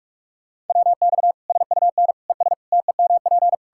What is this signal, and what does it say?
Morse code, heard as a keyed tone, about 46 wpm, 700 Hz. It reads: WX SUN ES TEMP
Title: Practice QSO - Exchange 2 - Info - 45wpm